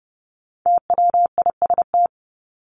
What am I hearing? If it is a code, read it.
Morse code, heard as a keyed tone, about 30 wpm, 700 Hz. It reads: TWIST
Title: Top 1751-2000 Words 30wpm